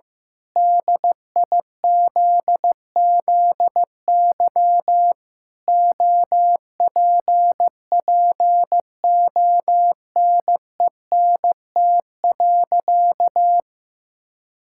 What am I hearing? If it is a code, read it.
Morse code, heard as a keyed tone, about 15 wpm, 700 Hz. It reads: DIZZY OPPONENT.
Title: Pangram Challenge v2 - 15wpm